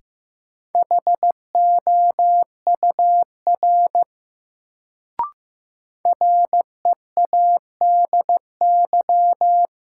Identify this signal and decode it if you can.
Morse code, heard as a keyed tone, about 15 wpm, 700 Hz. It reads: HOUR READY
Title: Sentences from Top 400 Words - New Word Drill - 15wpm